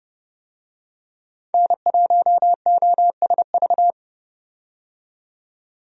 Morse code, heard as a keyed tone, about 31 wpm, 700 Hz. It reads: N1OSV
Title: USA Call Signs - 5 Characters - 30wpm